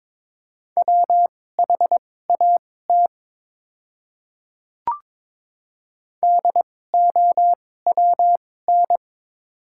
Morse code, heard as a keyed tone, about 22 wpm, spaced as for 16 wpm, 700 Hz. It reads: WHAT DOWN
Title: 2x Character Spacing - Top 300 Words - 22wpm